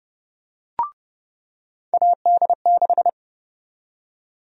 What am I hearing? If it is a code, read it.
Morse code, heard as a keyed tone, about 30 wpm, 700 Hz. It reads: AD6
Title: USA Call Sign Prefixes - 30wpm